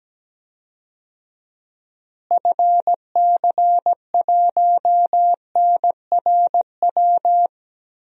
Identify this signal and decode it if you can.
Morse code, heard as a keyed tone, about 17 wpm, 700 Hz. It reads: FC1NRW
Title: DXCC-100 Call Signs - 6 Characters - Repeated 3x - 17wpm